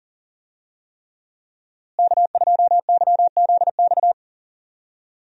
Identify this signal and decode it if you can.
Morse code, heard as a keyed tone, about 39 wpm, 700 Hz. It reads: K2YZX
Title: USA Call Signs - 5 Characters - Repeated 3x - 40wpm